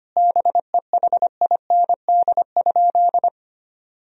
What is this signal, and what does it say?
Morse code, heard as a keyed tone, about 25 wpm, 700 Hz. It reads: BEHIND?
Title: Sentences from Top 1000 Words - Review Part 2 - 25wpm